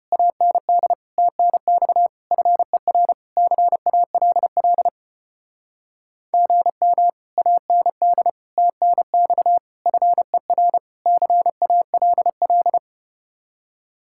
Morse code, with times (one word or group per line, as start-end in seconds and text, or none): 0.1-0.9 AND
1.2-2.1 TNX
2.3-3.1 FER
3.4-4.9 CALL
6.3-7.1 GM
7.4-8.3 AND
8.6-9.6 TNX
9.9-10.8 FER
11.1-12.8 CALL